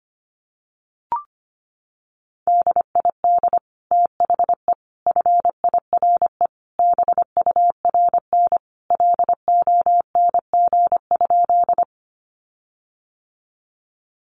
Morse code, heard as a keyed tone, about 25 wpm, 700 Hz. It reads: DID THE FIRE BURN LONG?
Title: Sentences from Top 1300 Words 25wpm